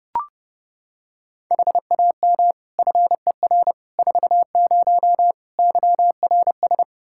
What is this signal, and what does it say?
Morse code, heard as a keyed tone, about 32 wpm, 700 Hz. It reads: HAM FER 40 YRS
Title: Practice QSO - Exchange 3 - More Info - 30wpm